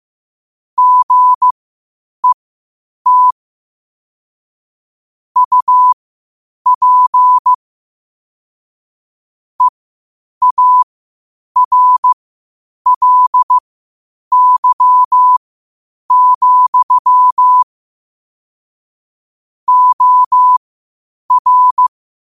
Morse code, heard as a keyed tone, about 15 wpm, spaced as for 8 wpm, 1000 Hz. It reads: GET UP EARLY, OR
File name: 8wpm